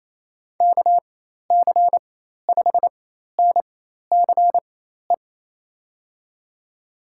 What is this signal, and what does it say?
Morse code, heard as a keyed tone, about 28 wpm, spaced as for 13 wpm, 700 Hz. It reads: KC5NCE
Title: USA Call Signs - 6 Characters - 4x Character Spacing - 28wpm